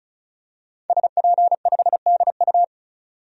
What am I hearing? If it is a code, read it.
Morse code, heard as a keyed tone, about 34 wpm, 700 Hz. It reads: SP5DU
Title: DXCC-100 Call Signs - 5 Characters - 35wpm